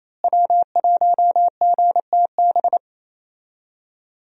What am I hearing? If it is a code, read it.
Morse code, heard as a keyed tone, about 28 wpm, 700 Hz. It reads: W1GTB
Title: USA Call Signs - 5 Characters - Repeated 2x - 28wpm